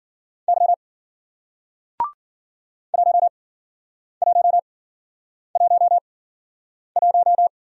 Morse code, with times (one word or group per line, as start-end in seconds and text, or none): 0.5-0.7 Y
2.9-7.5 1111